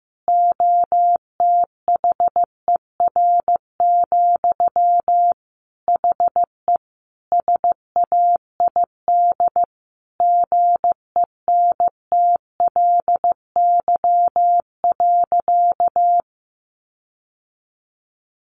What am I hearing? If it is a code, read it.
Morse code, heard as a keyed tone, about 15 wpm, 700 Hz. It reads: OTHER, HE SAID GENTLY.
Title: Sentences from Top 1500 Words 15wpm